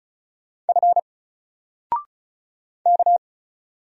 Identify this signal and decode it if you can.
Morse code, heard as a keyed tone, about 36 wpm, 700 Hz. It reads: F K
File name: Single-Letter-Number-35wpm